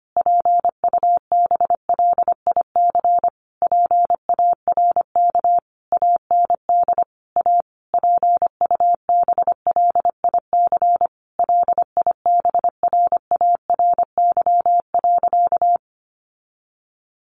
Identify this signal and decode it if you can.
Morse code, heard as a keyed tone, about 25 wpm, 700 Hz. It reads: PUBLIC PARK AND A PUBLIC LIBRARY.
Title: Sentences from Top 1300 Words 25wpm